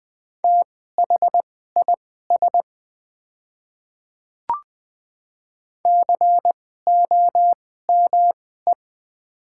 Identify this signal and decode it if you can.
Morse code, heard as a keyed tone, about 20 wpm, spaced as for 15 wpm, 700 Hz. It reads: THIS COME
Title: 2x Character Spacing - Top 100 Words - 20wpm